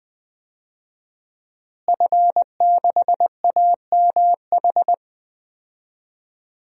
Morse code, heard as a keyed tone, about 19 wpm, 700 Hz. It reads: F6AMH
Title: DXCC-100 Call Signs - 5 Characters - Repeated 2x - 20wpm